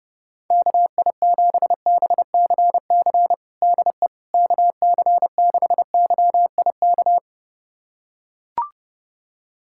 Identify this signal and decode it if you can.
Morse code, heard as a keyed tone, about 32 wpm, 700 Hz. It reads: KI7BCC DE KC6YIK
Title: USA Callsign DE Callsign 30wpm